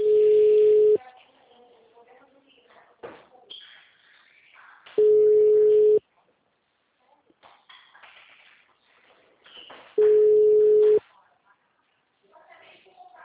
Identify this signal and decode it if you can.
Ringback tone (Europe, 425 Hz), with speech.